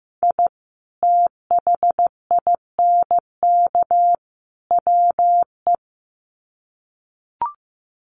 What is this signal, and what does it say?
Morse code, heard as a keyed tone, about 15 wpm, 700 Hz. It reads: I THINK WE
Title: Sets of 3 Words 15wpm